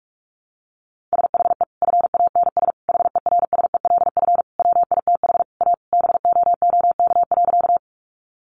Morse code, heard as a keyed tone, about 44 wpm, 700 Hz. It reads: SHE FANS HERSELF WITH A BOOK.
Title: Additional Easy Phrases 45wpm